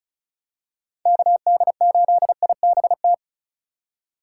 Morse code, heard as a keyed tone, about 34 wpm, 700 Hz. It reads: KD8IBT
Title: USA Call Signs - 6 Characters - 4x Character Spacing - 35wpm